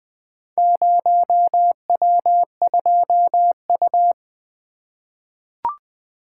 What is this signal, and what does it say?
Morse code, heard as a keyed tone, about 21 wpm, 700 Hz. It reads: 0W2U
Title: Four Letter-Number Combinations - 20wpm